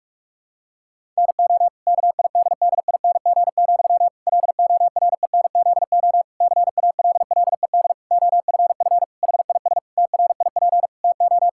Morse code, heard as a keyed tone, about 44 wpm, 700 Hz. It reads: NO KIDDING, LORENZO CALLED OFF HIS TRIP TO
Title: Pangram Challenge v2 - 45wpm